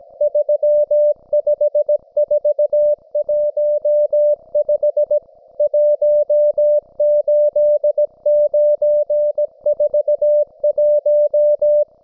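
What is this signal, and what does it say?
Morse code, heard as a keyed tone, about 17 wpm, 590 Hz. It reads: 35415 18941
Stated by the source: Russian military Morse station, M32, RDL, on 5268 kHz.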